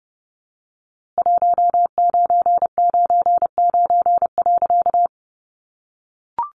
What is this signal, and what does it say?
Morse code, heard as a keyed tone, about 30 wpm, 700 Hz. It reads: 1999.
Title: A Sound Of Thunder 30wpm